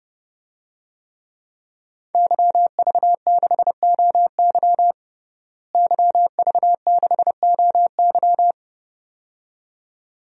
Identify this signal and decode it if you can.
Morse code, heard as a keyed tone, about 30 wpm, 700 Hz. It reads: YV6OY YV6OY
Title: DXCC-100 Call Signs - 5 Characters - Repeated 2x - 30wpm